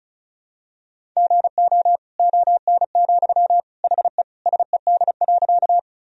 Morse code, heard as a keyed tone, about 36 wpm, 700 Hz. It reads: GO ON, HE SED.
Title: Sentences from Top 100 Words and Abbreviations 35wpm